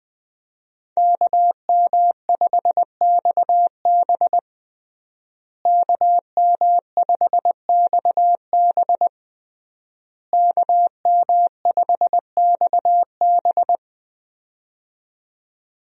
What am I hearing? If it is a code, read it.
Morse code, heard as a keyed tone, about 21 wpm, 700 Hz. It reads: KM5XB KM5XB KM5XB
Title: USA Call Signs - 5 Characters - Repeated 3x - 20wpm